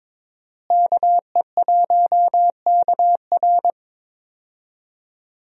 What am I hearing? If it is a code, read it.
Morse code, heard as a keyed tone, about 22 wpm, 700 Hz. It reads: KE1KR
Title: USA Call Signs - 5 Characters - Repeated 2x - 22wpm